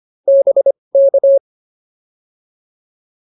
Morse code, heard as a keyed tone, about 25 wpm, spaced as for 20 wpm, 590 Hz. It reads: BK